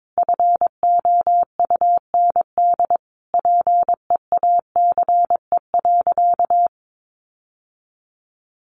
Morse code, heard as a keyed tone, about 22 wpm, 700 Hz. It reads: FOUND PEACE.
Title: Sentences from Top 1500 Words 22wpm